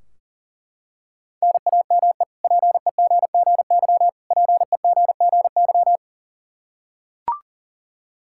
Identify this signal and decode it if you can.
Morse code, heard as a keyed tone, about 40 wpm, 700 Hz. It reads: NAME PEGGY PEGGY
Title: K1USN SST QSO Practice 40wpm